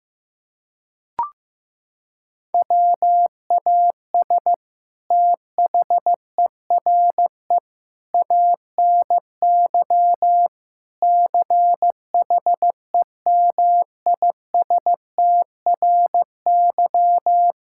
Morse code, heard as a keyed tone, about 15 wpm, 700 Hz. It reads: WAS THERE ANY CHEMISTRY
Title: Sentences from Top 1000 Words - Review Part 6 - 15wpm